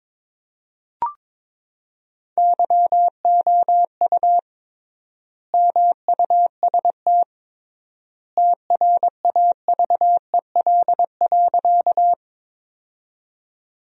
Morse code, heard as a keyed tone, about 22 wpm, 700 Hz. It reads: YOU MUST TRAVEL.
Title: 3x Word-Spacing - Sentences from Top 400 Words - 22wpm